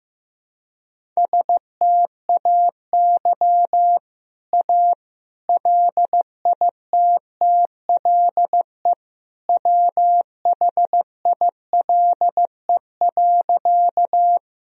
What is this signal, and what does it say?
Morse code, heard as a keyed tone, about 15 wpm, 700 Hz. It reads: STAY A LITTLE WHILE.
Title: Sentences from Top 1000 Words - Review Part 1 - 15wpm